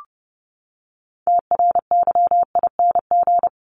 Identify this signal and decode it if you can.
Morse code, heard as a keyed tone, about 30 wpm, 700 Hz. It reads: TRYING
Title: ING Suffix - Top 100 Words with Base Words 30wpm